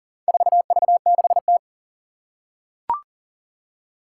Morse code, heard as a keyed tone, about 40 wpm, 700 Hz. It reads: 4V6T
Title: Four Challenging Letter-Number Combinations 40wpm